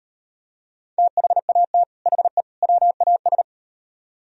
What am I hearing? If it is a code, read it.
Morse code, heard as a keyed tone, about 38 wpm, 700 Hz. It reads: THAT HE WAS